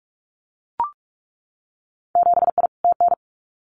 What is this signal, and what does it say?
Morse code, heard as a keyed tone, about 44 wpm, 700 Hz. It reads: 7I TN
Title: Winter Field Day Exchanges Encore 45wpm